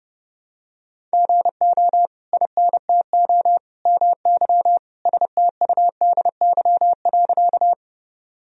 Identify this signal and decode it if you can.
Morse code, heard as a keyed tone, about 30 wpm, 700 Hz. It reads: GO INTO MY STUDY.
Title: Sentences from Top 1000 Words - Review Part 1 - 30wpm